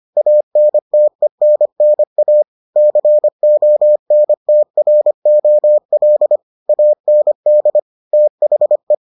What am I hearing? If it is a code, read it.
Morse code, heard as a keyed tone, about 25 wpm, 600 Hz. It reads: ANTENNA CONTROL AND THE